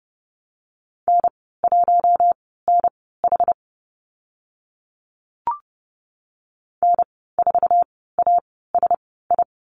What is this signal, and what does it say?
Morse code, heard as a keyed tone, about 30 wpm, spaced as for 17 wpm, 700 Hz. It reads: N1NH N4ASI
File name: Lesson 14g - Number 2 - Call Signs 30wpm